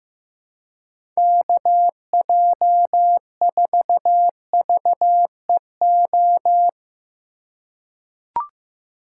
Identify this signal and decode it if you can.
Morse code, heard as a keyed tone, about 16 wpm, 700 Hz. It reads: KJ4VEO
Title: USA Call Signs - 6 Characters - 2x Character Spacing - 15wpm